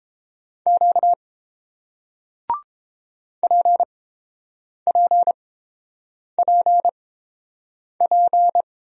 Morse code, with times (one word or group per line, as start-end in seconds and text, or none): 0.7-1.1 Q
3.4-8.6 PPPP